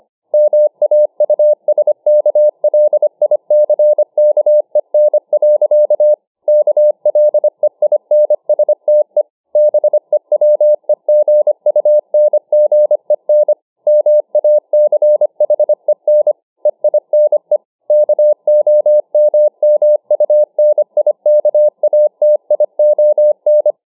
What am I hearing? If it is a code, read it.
Morse code, heard as a keyed tone, about 25 wpm, 600 Hz. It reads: MAUSKLICKEN. KLEINSTE BEWEGUNGEN MACHEN EINE KOMMUNIKATION